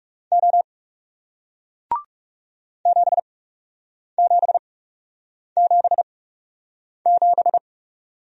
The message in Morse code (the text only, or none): O 7777